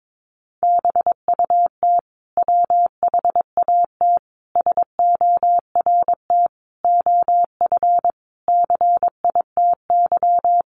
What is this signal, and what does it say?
Morse code, heard as a keyed tone, about 22 wpm, 700 Hz. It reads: BUT WHAT SORT OF CITY
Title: Sentences from Top 1300 Words 22wpm